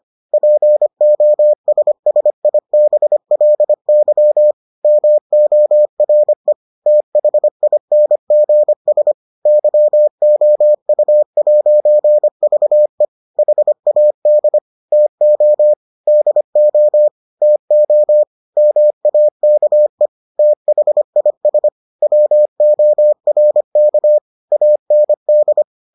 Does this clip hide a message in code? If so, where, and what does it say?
Yes: Morse code from 0.3 to 25.6 s, heard as a keyed tone, about 25 wpm, 600 Hz. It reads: POSSIBLY MORE THINGS YOU'VE HAD TO DO TO MAKE THIS WORK AND